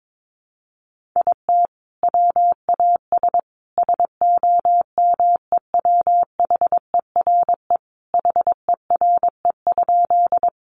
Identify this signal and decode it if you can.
Morse code, heard as a keyed tone, about 22 wpm, 700 Hz. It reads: IT WAS SOMEWHERE HERE?
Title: Sentences from Top 1400 Words 22wpm